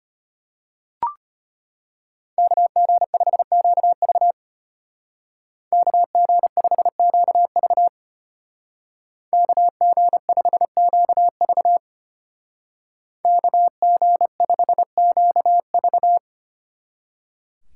Morse code, heard as a keyed tone, about 33 wpm, 700 Hz. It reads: KG5QV KG5QV KG5QV KG5QV